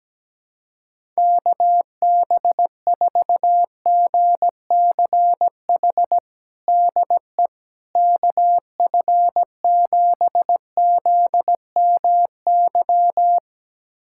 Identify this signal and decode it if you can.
Morse code, heard as a keyed tone, about 17 wpm, 700 Hz. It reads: KB4GCH DE KF7ZMY
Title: USA Callsign DE Callsign 17wpm